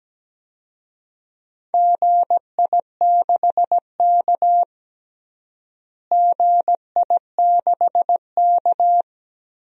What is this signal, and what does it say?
Morse code, heard as a keyed tone, about 17 wpm, 700 Hz. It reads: GI6K GI6K
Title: DXCC-100 Call Signs - 4 Characters - Repeated 2x - 17wpm